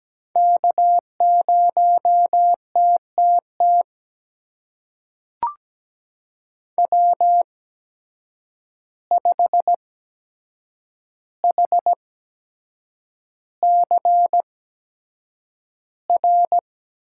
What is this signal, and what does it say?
Morse code, heard as a keyed tone, about 17 wpm, spaced as for 7 wpm, 700 Hz. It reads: K0TTT W5HCR
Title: USA Call Signs - 5 Characters - 8x Character Spacing - 17wpm